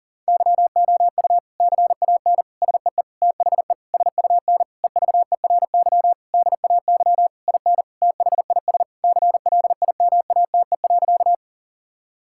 Morse code, heard as a keyed tone, about 40 wpm, 700 Hz. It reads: YOU CAN SEE THE SUN EVERY DAY IN THIS CLIMATE.
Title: Sentences from Top 1600 Words 40wpm